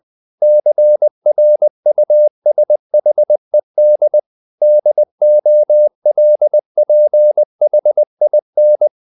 Morse code, heard as a keyed tone, about 20 wpm, 600 Hz. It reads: CRUSHED DOLPHIN
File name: the old man and the sea part 2 20wpm